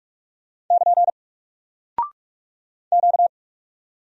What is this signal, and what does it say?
Morse code, heard as a keyed tone, about 44 wpm, 700 Hz. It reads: <KN> Q
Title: Single Character (Letter:Number:Punctuation:Prosign) Alt 45wpm